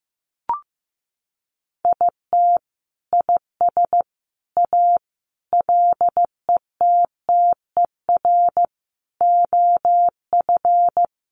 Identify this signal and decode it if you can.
Morse code, heard as a keyed tone, about 14 wpm, 700 Hz. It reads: IT IS A LETTER OF
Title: Sentences from Top 1400 Words 15wpm